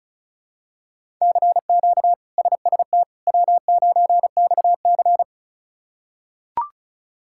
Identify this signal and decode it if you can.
Morse code, heard as a keyed tone, about 34 wpm, 700 Hz. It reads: CQ SST W9XC
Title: K1USN SST QSO Practice - 35wpm